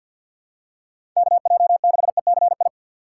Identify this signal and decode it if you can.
Morse code, heard as a keyed tone, about 47 wpm, 700 Hz. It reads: KJ6ECI
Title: USA Call Signs - 6 Characters - 3x Character Spacing - 50wpm